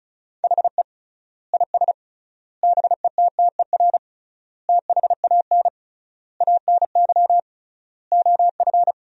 Morse code, heard as a keyed tone, about 34 wpm, 700 Hz. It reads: HE IS BETTER THAN ANY OF
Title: 3x Word-Spacing - Sentences from Top 400 Words - 35wpm